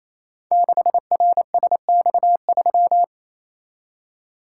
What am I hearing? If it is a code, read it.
Morse code, heard as a keyed tone, about 29 wpm, 700 Hz. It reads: 6RSX3
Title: Code Groups - 5 Letter:Numbers - 28wpm